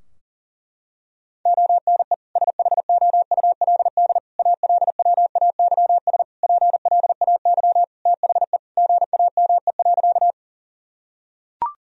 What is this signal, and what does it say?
Morse code, heard as a keyed tone, about 40 wpm, 700 Hz. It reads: ONE SHOULD ALWAYS PLAY THE GAME.
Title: Sentences from Top 1000 Words - Review Part 3 - 40wpm